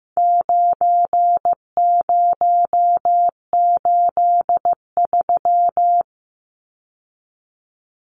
Morse code, heard as a keyed tone, about 15 wpm, 700 Hz. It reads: 9083
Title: Numbers - 4-Digit 15wpm